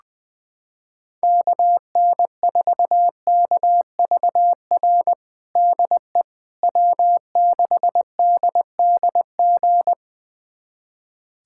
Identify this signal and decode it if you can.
Morse code, heard as a keyed tone, about 21 wpm, 700 Hz. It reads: KN4KVR DE W6DDG
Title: USA Callsign DE Callsign 20wpm